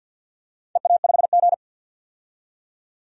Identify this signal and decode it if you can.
Morse code, heard as a keyed tone, about 52 wpm, 700 Hz. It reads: EA5G
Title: DXCC-100 Call Signs - 4 Characters - 50wpm